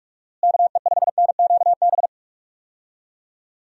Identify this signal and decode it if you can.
Morse code, heard as a keyed tone, about 43 wpm, 700 Hz. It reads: KE5NQB
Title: USA Call Signs - 6 Characters - Repeated 2x - 45wpm